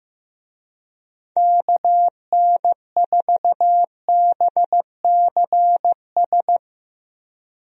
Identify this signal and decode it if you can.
Morse code, heard as a keyed tone, about 16 wpm, 700 Hz. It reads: KN4BCS
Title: USA Call Signs - 6 Characters - Repeated 2x - 15wpm